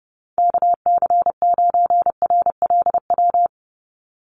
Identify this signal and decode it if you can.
Morse code, heard as a keyed tone, about 30 wpm, 700 Hz. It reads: KC9RLW
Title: USA Call Signs 30wpm